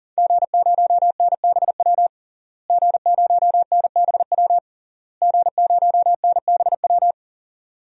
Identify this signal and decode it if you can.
Morse code, heard as a keyed tone, about 40 wpm, 700 Hz. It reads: G0NBW G0NBW G0NBW
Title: DXCC-100 Call Signs - 5 Characters - Repeated 3x - 40wpm